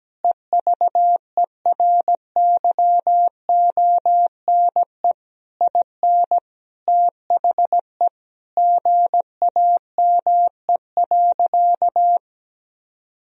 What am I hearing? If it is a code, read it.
Morse code, heard as a keyed tone, about 17 wpm, 700 Hz. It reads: EVERYONE IN THE GAME.
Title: Sentences from Top 1000 Words - Review Part 4 - 17wpm